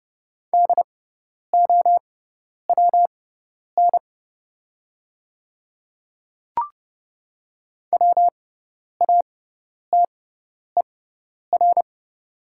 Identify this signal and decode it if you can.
Morse code, heard as a keyed tone, about 30 wpm, spaced as for 10 wpm, 700 Hz. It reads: DOWN WATER
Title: 6x Character Spacing - Top 100 Words - 30wpm